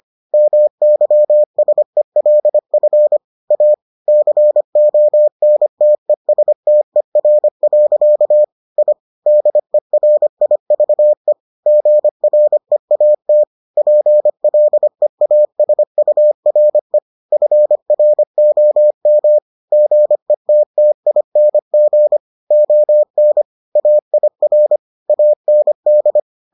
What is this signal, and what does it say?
Morse code, heard as a keyed tone, about 25 wpm, 600 Hz. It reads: MYSELF A CONTESTER. I DERIVE GREAT PLEASURE FROM GETTING ON AIR AND